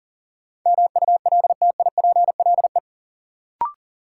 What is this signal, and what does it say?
Morse code, heard as a keyed tone, about 37 wpm, 700 Hz. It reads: MULTIPLE
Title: Sets of 5 Words - New Word Drill - 40wpm